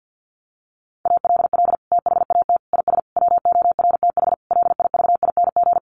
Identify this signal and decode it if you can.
Morse code, heard as a keyed tone, about 50 wpm, 700 Hz. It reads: ALL THAT IS WORTH LIVING
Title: Sentences from Top 1100 Words 50wpm